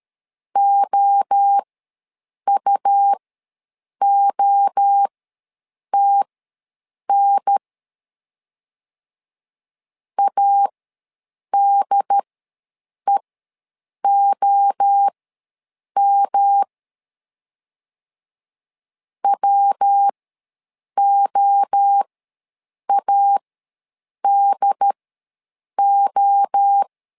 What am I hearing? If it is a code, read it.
Morse code, heard as a keyed tone, about 13 wpm, spaced as for 7 wpm, 800 Hz. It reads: OUOTN ADEOM WOADO
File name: LEZIONI / 3 MWRO / 3 MWRO ws 12wpm 1914